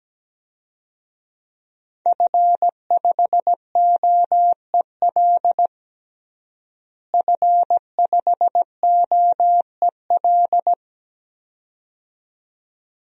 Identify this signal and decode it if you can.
Morse code, heard as a keyed tone, about 17 wpm, 700 Hz. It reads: F5OEL F5OEL
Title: DXCC-100 Call Signs - 5 Characters - Repeated 2x - 17wpm